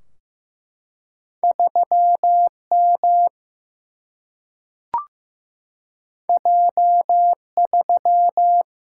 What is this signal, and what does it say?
Morse code, heard as a keyed tone, about 15 wpm, 700 Hz. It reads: 3M J3
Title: Single-Letter Single-Number 15wpm